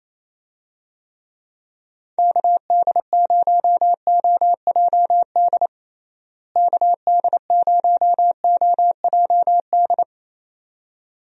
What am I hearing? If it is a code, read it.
Morse code, heard as a keyed tone, about 28 wpm, 700 Hz. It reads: KD0OJD KD0OJD
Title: USA Call Signs - 6 Characters - Repeated 2x - 28wpm